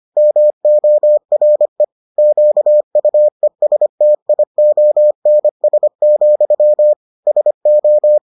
Morse code, heard as a keyed tone, about 25 wpm, 600 Hz. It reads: MORE QUESTIONS, SO